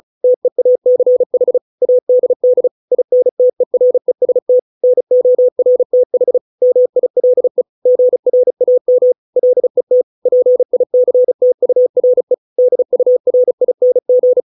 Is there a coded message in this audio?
Yes: Morse code from 0.2 to 14.4 s, heard as a keyed tone, about 34 wpm, 500 Hz. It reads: TEACH ADD INTEREST NORTH MILE GRAM LET PICTURE DURING